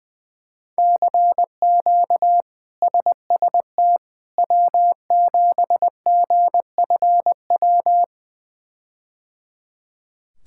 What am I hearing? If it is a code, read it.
Morse code, heard as a keyed tone, about 20 wpm, 700 Hz. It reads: CQ SST W7GFW
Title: K1USN SST QSO Practice - 20wpm